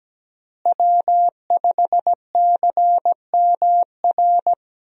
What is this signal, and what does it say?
Morse code, heard as a keyed tone, about 17 wpm, 700 Hz. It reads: W5CMR
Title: USA Call Signs - 5 Characters - Repeated 3x - 17wpm